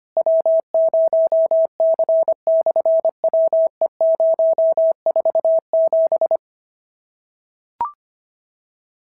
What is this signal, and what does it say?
Morse code, heard as a keyed tone, about 25 wpm, 650 Hz. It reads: W0C/WE047
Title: SOTA QSO Practice 25wpm